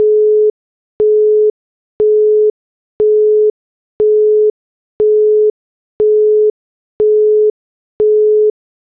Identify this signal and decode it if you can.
Busy tone (Europe, 425 Hz).
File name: Motorola / BusyTone